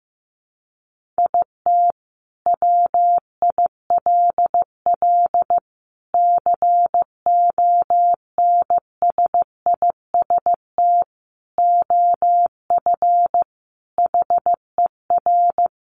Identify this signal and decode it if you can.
Morse code, heard as a keyed tone, about 15 wpm, 700 Hz. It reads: IT WILL CONSIST OF HER
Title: Sentences from Top 1400 Words 15wpm